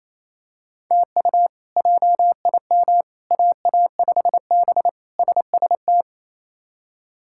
Morse code, heard as a keyed tone, about 27 wpm, 700 Hz. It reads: TU JIM AA5B SST
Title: K1USN SST QSO Practice - 28wpm